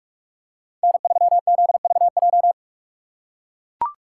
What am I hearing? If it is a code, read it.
Morse code, heard as a keyed tone, about 44 wpm, 700 Hz. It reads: N3ZVJ